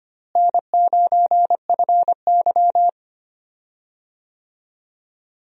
Morse code, heard as a keyed tone, about 25 wpm, 700 Hz. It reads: N9FY
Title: USA Call Signs - 4 Characters - Repeated 2x - 25wpm